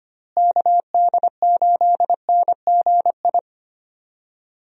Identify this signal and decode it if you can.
Morse code, heard as a keyed tone, about 25 wpm, 700 Hz. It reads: KD8NGI
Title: USA Call Signs - 6 Characters - Repeated 2x - 25wpm